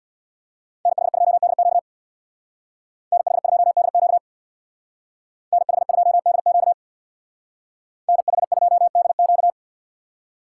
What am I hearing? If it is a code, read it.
Morse code, heard as a keyed tone, about 64 wpm, 700 Hz. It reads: NH2DQ NH2DQ NH2DQ NH2DQ